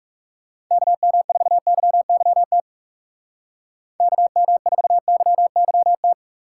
Morse code, heard as a keyed tone, about 35 wpm, 700 Hz. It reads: KM4YYT KM4YYT
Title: Speed Racing - USA 2x3 Call Signs 30wpm